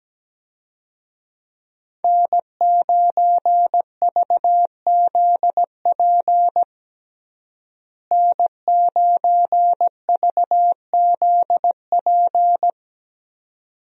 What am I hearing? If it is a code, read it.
Morse code, heard as a keyed tone, about 17 wpm, 700 Hz. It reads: N9VZP N9VZP
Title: USA Call Signs - 5 Characters - Repeated 2x - 17wpm